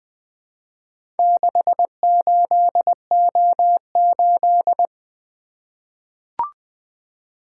Morse code, heard as a keyed tone, about 20 wpm, 700 Hz. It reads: 68O8
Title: Four Letter-Number Combinations - 20wpm